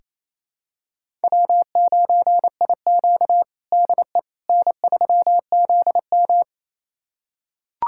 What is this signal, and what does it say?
Morse code, heard as a keyed tone, about 28 wpm, 700 Hz. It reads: W9IQ DE N3ZM
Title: USA Callsign DE Callsign - 4 Characters - 28wpm